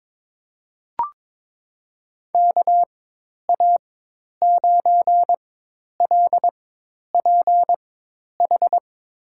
Morse code, heard as a keyed tone, about 22 wpm, spaced as for 10 wpm, 700 Hz. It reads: KA9LPH
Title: USA Call Signs - 6 Characters - 4x Character Spacing - 22wpm